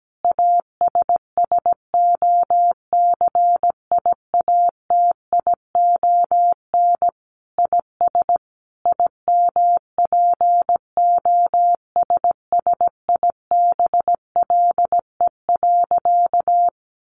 Morse code, heard as a keyed tone, about 17 wpm, 700 Hz. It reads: ASSOCIATION IS IMPOSSIBLE.
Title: Sentences from Top 1100 Words 17wpm